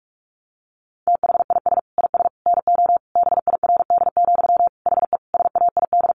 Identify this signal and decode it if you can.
Morse code, heard as a keyed tone, about 44 wpm, 700 Hz. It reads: THIS IS NO BIRD, HE SAID
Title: Sentences from Top 700 Words 45wpm